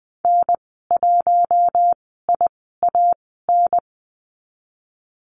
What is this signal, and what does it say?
Morse code, heard as a keyed tone, about 19 wpm, spaced as for 14 wpm, 700 Hz. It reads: N1IAN
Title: 048 Lesson 9g - Number 4 - Call Signs 20wpm